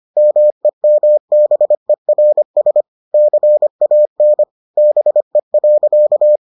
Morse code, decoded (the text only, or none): MEMBERS CAN BE.